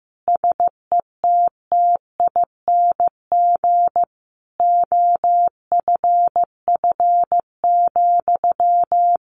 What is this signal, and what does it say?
Morse code, heard as a keyed tone, about 15 wpm, 700 Hz. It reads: SETTING OFF,
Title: Sentences from Top 1300 Words 15wpm